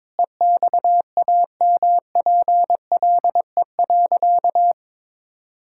Morse code, heard as a keyed tone, about 22 wpm, 700 Hz. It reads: EXAMPLE.
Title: Sentences from Top 1000 Words - Review Part 6 - 22wpm